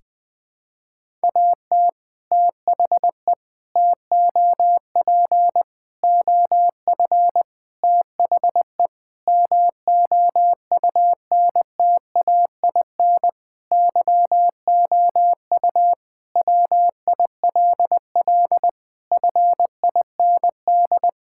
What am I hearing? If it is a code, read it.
Morse code, heard as a keyed tone, about 19 wpm, 700 Hz. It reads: AT THE TOP OF THE MOUNTAIN YOU WILL FIND
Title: Sentences with 3-Word Phrases - 20wpm